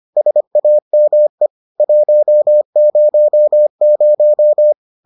Morse code, heard as a keyed tone, about 25 wpm, 600 Hz. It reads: SAME 100